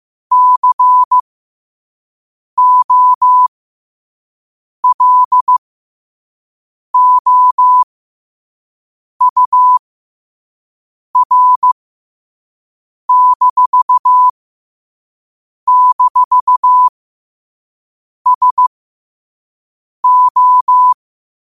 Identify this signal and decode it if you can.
Morse code, heard as a keyed tone, about 15 wpm, spaced as for 5 wpm, 1000 Hz. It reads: COLOUR--SO
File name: qotdfiles / 5wpm